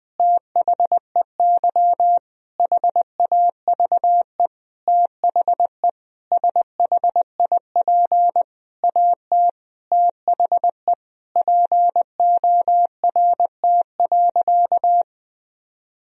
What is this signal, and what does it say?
Morse code, heard as a keyed tone, about 20 wpm, 700 Hz. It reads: THEY HAVE THE SHIP AT THE PORT.
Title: Sentences from Top 1000 Words - Review Part 2 - 20wpm